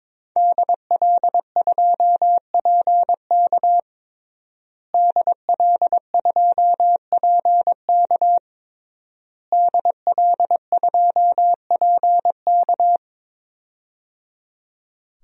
Morse code, heard as a keyed tone, about 22 wpm, 700 Hz. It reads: DL2PK DL2PK DL2PK
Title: DXCC-100 Call Signs - 5 Characters - Repeated 3x - 22wpm